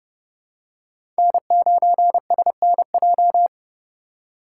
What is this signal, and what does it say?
Morse code, heard as a keyed tone, about 30 wpm, 700 Hz. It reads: N9SNJ
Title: USA Call Signs - 5 Characters - 4x Character Spacing - 30wpm